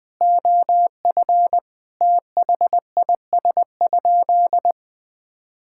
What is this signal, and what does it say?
Morse code, heard as a keyed tone, about 21 wpm, 700 Hz. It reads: OF THIS?
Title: Sentences from Top 1000 Words - Review Part 6 - 20wpm